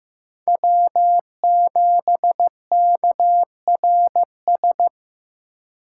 Morse code, heard as a keyed tone, about 15 wpm, 700 Hz. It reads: W7KRS
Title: K1USN SST QSO Practice - 15wpm